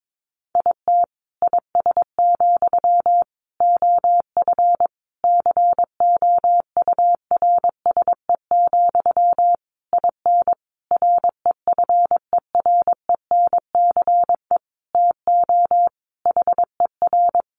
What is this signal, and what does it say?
Morse code, heard as a keyed tone, about 22 wpm, 700 Hz. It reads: IT IS, OF COURSE, IN REFERENCE TO HER
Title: Sentences from Top 1200 Words 22wpm